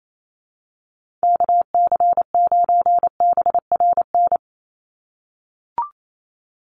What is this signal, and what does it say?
Morse code, heard as a keyed tone, about 28 wpm, 700 Hz. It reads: KC9BRN
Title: USA Call Signs 28wpm